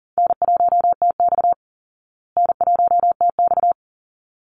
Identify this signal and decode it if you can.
Morse code, heard as a keyed tone, about 37 wpm, 700 Hz. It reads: N1TX N1TX
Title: Easy Call Signs Repeated 2x - 40wpm